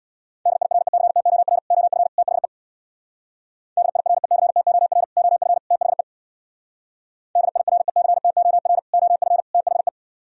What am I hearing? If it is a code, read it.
Morse code, heard as a keyed tone, about 64 wpm, 700 Hz. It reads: DIRECTOR OF THE DIRECTOR OF THE DIRECTOR OF THE